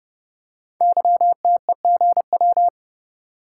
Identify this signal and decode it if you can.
Morse code, heard as a keyed tone, about 30 wpm, 700 Hz. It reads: YTEGW
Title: Code Groups - 5 Letter:Numbers - 30wpm